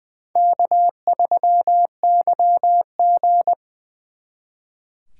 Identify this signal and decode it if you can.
Morse code, heard as a keyed tone, about 24 wpm, 700 Hz. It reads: K3YG
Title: Speed Racing - USA 1x2 Call Signs 20wpm